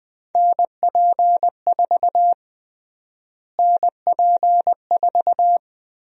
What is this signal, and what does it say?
Morse code, heard as a keyed tone, about 20 wpm, 700 Hz. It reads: NP4 NP4
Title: USA Call Sign Prefixes - Repeated 2x - 20wpm